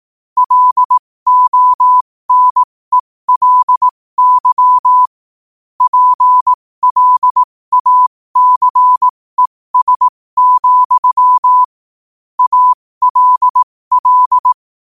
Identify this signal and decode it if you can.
Morse code, heard as a keyed tone, about 18 wpm, spaced as for 15 wpm, 1000 Hz. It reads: LONELY PLACES, ALL
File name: qotdfiles / 15wpm